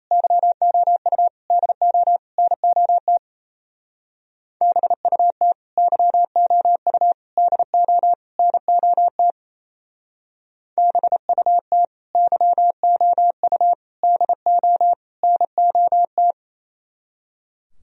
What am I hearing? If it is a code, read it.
Morse code, heard as a keyed tone, about 35 wpm, 700 Hz. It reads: YOU DO NOT BUT YOU DO NOT BUT YOU DO NOT
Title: Speed-Racing - Sets of 4 Words 28wpm